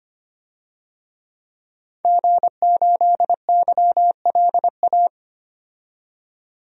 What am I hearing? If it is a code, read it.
Morse code, heard as a keyed tone, about 25 wpm, 700 Hz. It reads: G8YLA
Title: DXCC-100 Call Signs - 5 Characters - 25wpm